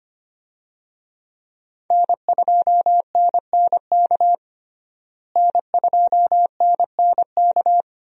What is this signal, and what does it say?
Morse code, heard as a keyed tone, about 25 wpm, 700 Hz. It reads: N2NNK N2NNK
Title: USA Call Signs - 5 Characters - Repeated 2x - 25wpm